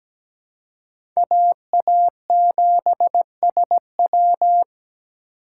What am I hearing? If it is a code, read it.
Morse code, heard as a keyed tone, about 17 wpm, 700 Hz. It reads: AA7SW
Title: USA Call Signs - 4x Character Spacing - 17wpm